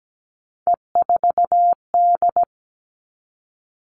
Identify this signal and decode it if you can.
Morse code, heard as a keyed tone, about 17 wpm, 700 Hz. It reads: E4D
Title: Three Challenging Letter-Number Combinations 17wpm